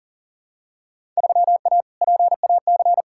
Morse code, heard as a keyed tone, about 40 wpm, 700 Hz. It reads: 3A PAC
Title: Field Day Exchanges Encore - 40wpm